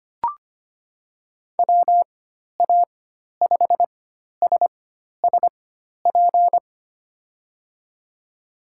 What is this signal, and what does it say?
Morse code, heard as a keyed tone, about 25 wpm, spaced as for 12 wpm, 700 Hz. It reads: WA5SSP
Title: USA Call Signs - 6 Characters - 4x Character Spacing - 25wpm